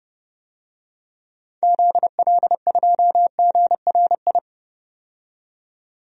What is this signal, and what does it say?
Morse code, heard as a keyed tone, about 30 wpm, 700 Hz. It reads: ZL2GRI
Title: DXCC-100 Call Signs - 6 Characters - 30wpm